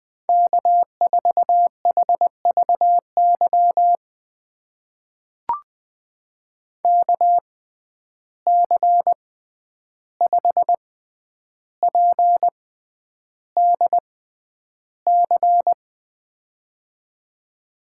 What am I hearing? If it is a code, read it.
Morse code, heard as a keyed tone, about 20 wpm, spaced as for 10 wpm, 700 Hz. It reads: K4HVY KC5PDC
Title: USA Call Signs - 6x Character Spacing - 20wpm